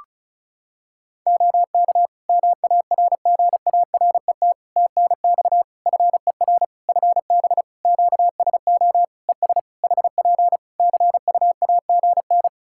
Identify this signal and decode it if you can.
Morse code, heard as a keyed tone, about 36 wpm, 700 Hz. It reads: OK MARGARET TNX FER FB QSO ES HP CUAGN
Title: Practice QSO - Exchange 4 - Ending - 35wpm